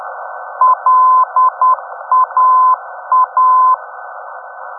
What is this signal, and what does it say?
Morse code, heard as a keyed tone, about 10 wpm, 1020 Hz. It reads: LAA
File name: europe / d-5078- dec 03 - ndb laa-ti